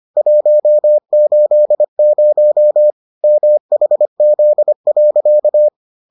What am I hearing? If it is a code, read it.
Morse code, heard as a keyed tone, about 25 wpm, 600 Hz. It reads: 180 MHZ.